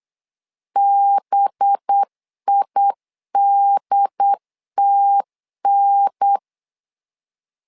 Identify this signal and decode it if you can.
Morse code, heard as a keyed tone, about 8 wpm, 800 Hz. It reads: BIDTN
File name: LEZIONI / 5 LBVC / 5 LB 8wpm 2816